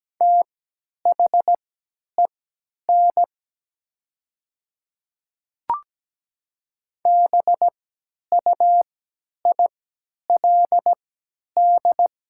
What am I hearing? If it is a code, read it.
Morse code, heard as a keyed tone, about 17 wpm, spaced as for 10 wpm, 700 Hz. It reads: THEN BUILD
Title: 3x Character Spacing - Top 200 Words - 17wpm